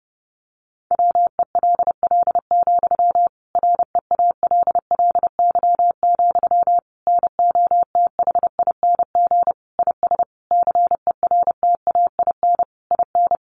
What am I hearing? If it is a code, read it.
Morse code, heard as a keyed tone, about 30 wpm, 700 Hz. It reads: WELL, REALLY, NOTHING IS CERTAIN IN
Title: Sentences from Top 600 Words 30wpm